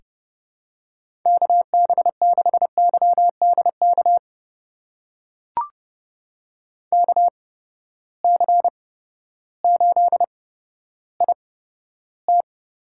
Morse code, heard as a keyed tone, about 31 wpm, spaced as for 14 wpm, 700 Hz. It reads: KB6YDK KC8IT
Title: USA Call Signs - 6 Characters - 8x Character Spacing - 30wpm